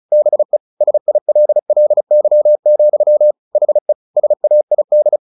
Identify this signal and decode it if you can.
Morse code, heard as a keyed tone, about 34 wpm, 600 Hz. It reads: BE SILLY, HE SAID
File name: the old man and the sea 2 35wpm